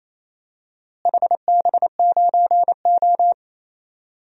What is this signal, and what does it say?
Morse code, heard as a keyed tone, about 28 wpm, 700 Hz. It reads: HB9O
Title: DXCC-100 Call Signs - 4 Characters - Repeated 3x - 28wpm